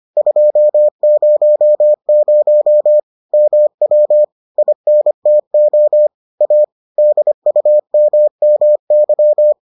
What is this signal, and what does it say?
Morse code, heard as a keyed tone, about 25 wpm, 600 Hz. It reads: 200 MW INTO A DUMMY